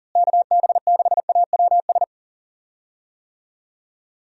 Morse code, heard as a keyed tone, about 37 wpm, 700 Hz. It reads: KB6AWS
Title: USA Call Signs - 6 Characters - Repeated 3x - 40wpm